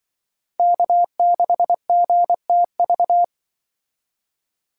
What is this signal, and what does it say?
Morse code, heard as a keyed tone, about 24 wpm, 700 Hz. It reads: K6GTV
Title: Speed Racing - USA 1x3 Call Signs 20wpm